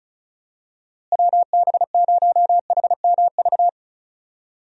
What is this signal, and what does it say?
Morse code, heard as a keyed tone, about 36 wpm, 700 Hz. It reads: WB0HMV
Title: USA Call Signs - 6 Characters - Repeated 3x - 35wpm